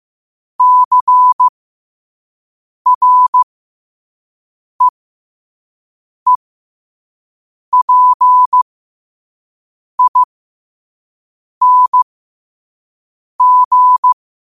Morse code, heard as a keyed tone, about 15 wpm, spaced as for 5 wpm, 1000 Hz. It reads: CREEPING
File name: qotdfiles / 5wpm